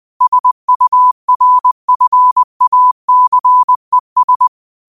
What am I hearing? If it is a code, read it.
Morse code, heard as a keyed tone, about 20 wpm, 1000 Hz. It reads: SURFACES
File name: qotdfiles / 20wpm